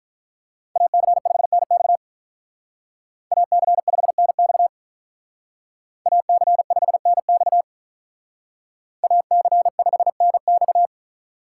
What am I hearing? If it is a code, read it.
Morse code, heard as a keyed tone, about 44 wpm, 700 Hz. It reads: AC5NX AC5NX AC5NX AC5NX
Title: Speed Racing - USA 2x2 Call Signs 35wpm